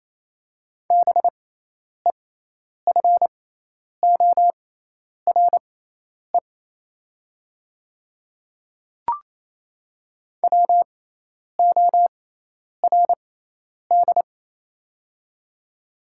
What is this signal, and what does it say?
Morse code, heard as a keyed tone, about 28 wpm, spaced as for 10 wpm, 690 Hz. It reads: BEFORE WORD
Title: 6x Character Spacing - Top 200 Words - 28wpm